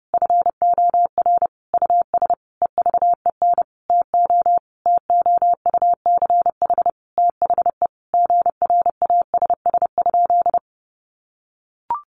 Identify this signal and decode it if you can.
Morse code, heard as a keyed tone, about 28 wpm, 700 Hz. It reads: FOR US EVEN TO TOUCH THE GRASS?
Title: A Sound Of Thunder 30wpm